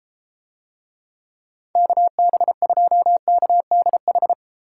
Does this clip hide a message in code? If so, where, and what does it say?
Yes: Morse code from 1.8 to 4.3 s, heard as a keyed tone, about 32 wpm, 700 Hz. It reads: KB2KDH